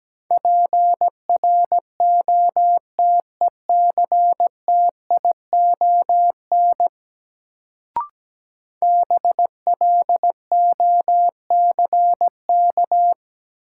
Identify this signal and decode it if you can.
Morse code, heard as a keyed tone, about 17 wpm, 700 Hz. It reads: PROTECTION BLOCK
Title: Sentences from Top 1600 Words - New Word Drill Alt - 17wpm